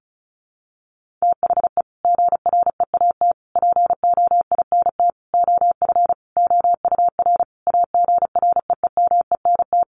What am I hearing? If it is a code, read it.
Morse code, heard as a keyed tone, about 34 wpm, 700 Hz. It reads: THE GREAT POINT OF OUR AGREEMENT
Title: Sentences from Top 1500 Words 35wpm